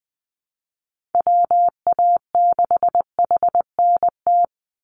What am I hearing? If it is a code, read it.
Morse code, heard as a keyed tone, about 20 wpm, 700 Hz. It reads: WA6HNT
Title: USA Call Signs 20wpm